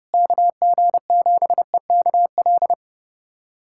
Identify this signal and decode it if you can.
Morse code, heard as a keyed tone, about 30 wpm, 700 Hz. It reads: KG7EKL
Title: USA Callsign DE Callsign 30wpm